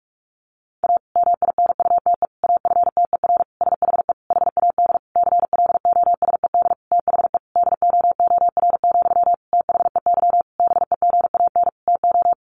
Morse code, heard as a keyed tone, about 44 wpm, 700 Hz. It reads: A MINUTE AFTER SHE HAD CLOSED THE DOOR, THEY BEGAN TO
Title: Sentences from Top 1400 Words 45wpm